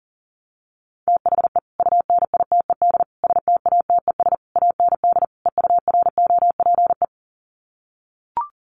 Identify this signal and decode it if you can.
Morse code, heard as a keyed tone, about 40 wpm, 700 Hz. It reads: THE UNITED STATES AND EUROPE
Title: Sets of 5 Words Encore 40wpm